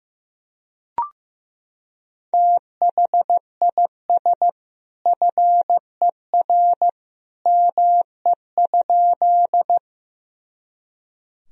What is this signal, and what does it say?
Morse code, heard as a keyed tone, about 15 wpm, 700 Hz. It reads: THIS FER ME?
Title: Sentences from Top 200 Words and Abbreviations 15wpm